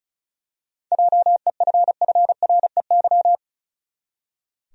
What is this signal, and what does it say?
Morse code, heard as a keyed tone, about 36 wpm, 700 Hz. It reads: JEFFREY
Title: USA Common Names - 35wpm